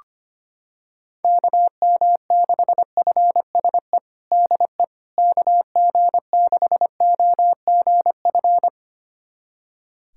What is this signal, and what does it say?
Morse code, heard as a keyed tone, about 25 wpm, 700 Hz. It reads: KM6FSE DE KG6OGF
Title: USA Callsign DE Callsign - 6 Characters - 25wpm